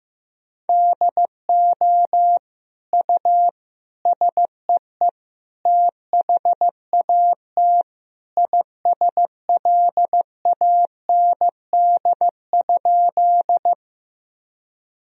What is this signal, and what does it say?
Morse code, heard as a keyed tone, about 15 wpm, 700 Hz. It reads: DO U SEE THAT ISLAND?
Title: Sentences from Top 500 Words and Abbreviations 15wpm